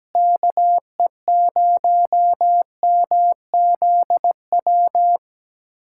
Morse code, heard as a keyed tone, about 17 wpm, 700 Hz. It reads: KE0MZW
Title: USA Callsign DE Callsign 17wpm